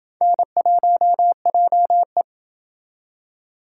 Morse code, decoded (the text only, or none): N1JE